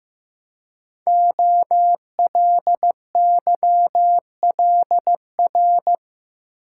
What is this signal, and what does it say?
Morse code, heard as a keyed tone, about 15 wpm, 700 Hz. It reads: OLYLR
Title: Code Groups - 5 Letter:Numbers - 15wpm